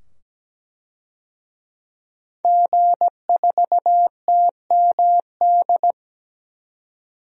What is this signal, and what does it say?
Morse code, heard as a keyed tone, about 17 wpm, 700 Hz. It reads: G4TMD
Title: DXCC-100 Call Signs - 5 Characters - 17wpm